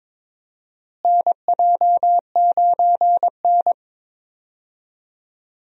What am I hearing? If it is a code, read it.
Morse code, heard as a keyed tone, about 22 wpm, 700 Hz. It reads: NJ9N
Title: USA Call Signs - 4 Characters - Repeated 2x - 22wpm